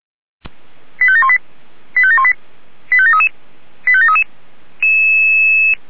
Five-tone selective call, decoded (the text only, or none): 85718 85718 85733 85733